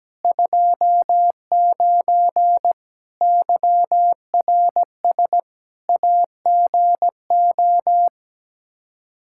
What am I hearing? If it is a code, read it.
Morse code, heard as a keyed tone, about 17 wpm, 700 Hz. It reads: 29 YRS AGO
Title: Practice QSO - Exchange 3 - More Info - 17wpm